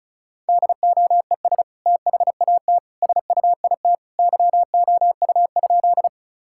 Morse code, heard as a keyed tone, about 36 wpm, 700 Hz. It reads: DOES THAT SUIT YOU?
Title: Sentences from Top 1600 Words 35wpm